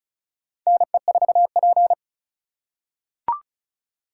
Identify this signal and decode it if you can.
Morse code, heard as a keyed tone, about 36 wpm, 700 Hz. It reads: NE4P
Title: USA Call Signs - 3x Character Spacing - 35wpm